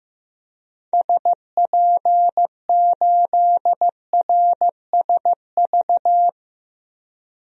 Morse code, heard as a keyed tone, about 15 wpm, 700 Hz. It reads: SP8RSV
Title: DXCC-100 Call Signs - 6 Characters - Repeated 2x - 15wpm